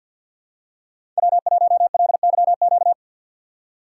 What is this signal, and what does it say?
Morse code, heard as a keyed tone, about 51 wpm, 700 Hz. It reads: W1LYQ
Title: USA Call Signs - 4x Character Spacing - 50wpm